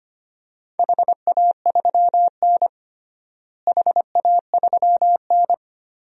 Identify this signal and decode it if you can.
Morse code, heard as a keyed tone, about 25 wpm, 700 Hz. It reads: HA3N HA3N
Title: DXCC-100 Call Signs - 4 Characters - Repeated 3x - 25wpm